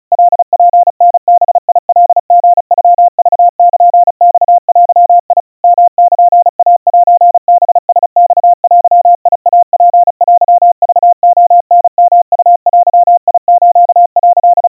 Morse code (Japanese Wabun code), heard as a keyed tone, about 36 wpm, 700 Hz. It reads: カツタバカリノクルマデ ヨルイセハラマデイツテクレタヨウデス」
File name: kawa / Morse sample / kamyu 130 cpm 700 Hz Crispy